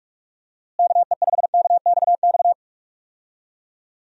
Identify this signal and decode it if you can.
Morse code, heard as a keyed tone, about 47 wpm, 700 Hz. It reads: KE5KXX
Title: USA Call Signs - 6 Characters - Repeated 2x - 45wpm